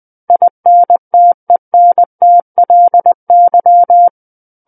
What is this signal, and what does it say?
Morse code, heard as a keyed tone, about 20 wpm, 700 Hz. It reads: INTENTLY
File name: 20 WPM